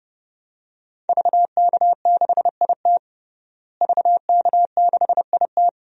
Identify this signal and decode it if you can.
Morse code, heard as a keyed tone, about 32 wpm, 700 Hz. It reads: VK6IT VK6IT
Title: DXCC-100 Call Signs - 5 Characters - Repeated 3x - 30wpm